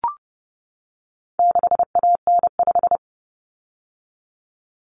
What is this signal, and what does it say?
Morse code, heard as a keyed tone, about 30 wpm, 700 Hz. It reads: 6AN5
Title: Common Cut Numbers - 4-Digit 30wpm